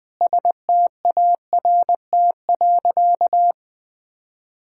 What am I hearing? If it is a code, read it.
Morse code, heard as a keyed tone, about 21 wpm, 700 Hz. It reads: START.
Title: Sentences with 7-Word Phrases - Spoken Beginning - 20wpm